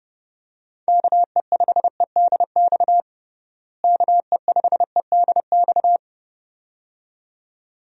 Morse code, heard as a keyed tone, about 30 wpm, 700 Hz. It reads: KE5EDX KE5EDX
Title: USA Call Signs - 6 Characters - Repeated 2x - 30wpm